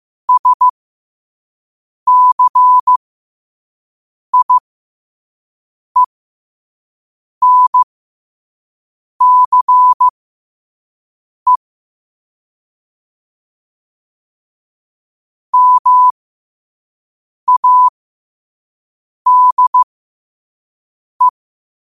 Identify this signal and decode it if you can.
Morse code, heard as a keyed tone, about 15 wpm, spaced as for 5 wpm, 1000 Hz. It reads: SCIENCE MADE
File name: qotdfiles / 5wpm